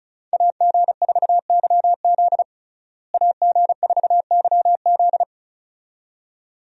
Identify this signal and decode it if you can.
Morse code, heard as a keyed tone, about 36 wpm, 700 Hz. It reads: AG4YZ AG4YZ
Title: USA Call Signs - 5 Characters - Repeated 2x - 35wpm